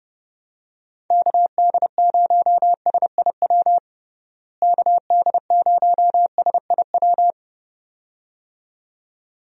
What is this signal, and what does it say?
Morse code, heard as a keyed tone, about 30 wpm, 700 Hz. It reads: KD0SIW KD0SIW
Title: USA Call Signs - 6 Characters - Repeated 2x - 30wpm